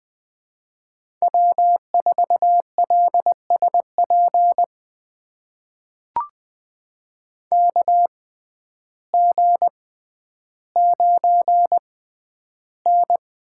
Morse code, heard as a keyed tone, about 21 wpm, spaced as for 11 wpm, 700 Hz. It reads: W4LSP KG9N
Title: USA Call Signs - 6x Character Spacing - 20wpm